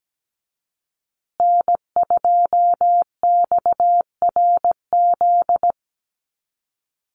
Morse code, heard as a keyed tone, about 17 wpm, 700 Hz. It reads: N2XRZ
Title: USA Call Signs Repeated 3x 17wpm